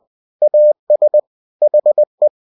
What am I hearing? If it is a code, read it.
Morse code, heard as a keyed tone, about 20 wpm, 600 Hz. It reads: AS HE
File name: cw-books / winnie-the-pooh / winnie-20wpm-page-0003